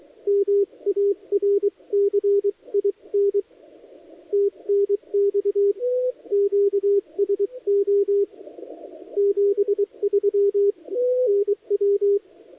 Morse code, heard as a keyed tone, about 23 wpm, 400 Hz. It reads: MARCIN TNX QSO 73 NW
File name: dlazaawansowanych / nagraniazpasma / titawaNETcontrol